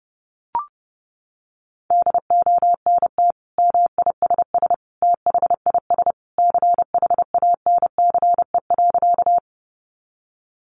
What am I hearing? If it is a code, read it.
Morse code, heard as a keyed tone, about 30 wpm, 700 Hz. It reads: DONT MISS THIS CHANCE.
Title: Sentences from Top 800 Words 30wpm